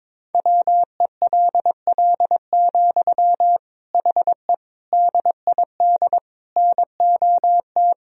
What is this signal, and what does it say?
Morse code, heard as a keyed tone, about 22 wpm, 700 Hz. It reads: WELL, HE DID NOT
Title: Sentences from Top 1000 Words - Review Part 2 - 22wpm